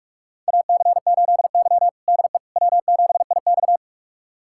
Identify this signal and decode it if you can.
Morse code, heard as a keyed tone, about 46 wpm, 700 Hz. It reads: AC8Y DE W7IX
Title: USA Callsign DE Callsign - 4 Characters - 45wpm